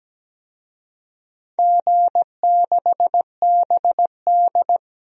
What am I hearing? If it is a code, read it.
Morse code, heard as a keyed tone, about 17 wpm, 700 Hz. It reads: G6BD
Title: DXCC-100 Call Signs - 4 Characters - 17wpm